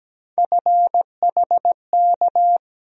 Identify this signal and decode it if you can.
Morse code, heard as a keyed tone, about 17 wpm, 700 Hz. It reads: FHK
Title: USA Call Sign Postfixes - 3 Letters 17wpm